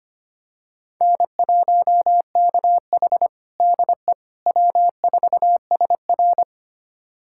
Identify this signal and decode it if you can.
Morse code, heard as a keyed tone, about 25 wpm, 700 Hz. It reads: N1KH DE W4SR
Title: USA Callsign DE Callsign - 4 Characters - 25wpm